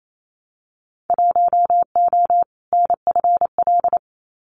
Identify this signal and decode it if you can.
Morse code, heard as a keyed tone, about 28 wpm, 700 Hz. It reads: 1O NFL
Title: Winter Field Day Exchanges 28wpm